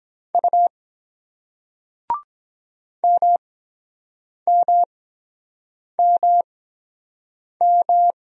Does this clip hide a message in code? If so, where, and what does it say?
Yes: Morse code from 0.3 to 8.1 s, heard as a keyed tone, about 23 wpm, spaced as for 7 wpm, 700 Hz. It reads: U MMMM